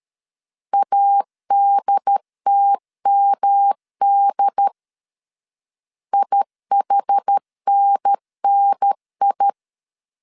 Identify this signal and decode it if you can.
Morse code, heard as a keyed tone, about 13 wpm, 800 Hz. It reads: ADTMD IHNNI
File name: CW / DET / 1 3 1500C S B 12wpm 2828